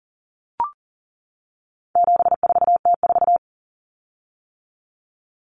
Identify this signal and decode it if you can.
Morse code, heard as a keyed tone, about 40 wpm, 700 Hz. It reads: 74T4
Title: Common Cut Numbers - 4-Digit 40wpm